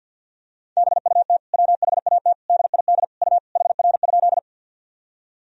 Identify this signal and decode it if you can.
Morse code, heard as a keyed tone, about 46 wpm, 700 Hz. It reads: BUT WHAT DID U HR?
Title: Sentences from Top 300 Words and Abbreviations 50wpm